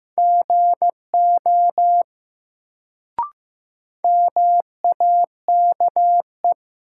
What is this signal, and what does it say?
Morse code, heard as a keyed tone, about 15 wpm, 700 Hz. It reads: GO MAKE
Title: Sentences from Top 100 Words - New Word Drill - 15wpm